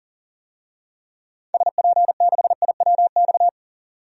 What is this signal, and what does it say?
Morse code, heard as a keyed tone, about 40 wpm, 700 Hz. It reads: SP6IWX
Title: DXCC-100 Call Signs - 6 Characters - 40wpm